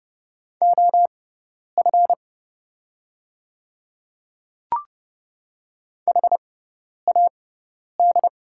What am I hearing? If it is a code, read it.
Morse code, heard as a keyed tone, about 30 wpm, spaced as for 10 wpm, 700 Hz. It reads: OF HAD
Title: 6x Character Spacing - Top 200 Words - 30wpm